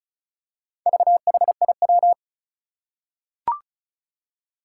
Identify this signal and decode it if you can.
Morse code, heard as a keyed tone, about 36 wpm, 700 Hz. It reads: VHIW
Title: Four Challenging Letter-Number Combinations 35wpm